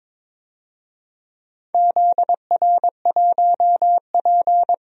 Morse code, heard as a keyed tone, about 22 wpm, 700 Hz. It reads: ZR1P
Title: DXCC-100 Call Signs - 4 Characters - 22wpm